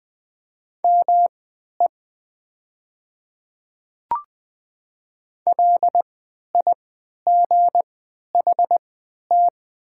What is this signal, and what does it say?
Morse code, heard as a keyed tone, about 20 wpm, spaced as for 11 wpm, 690 Hz. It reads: ME LIGHT
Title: 3x Character Spacing - Top 200 Words - 20wpm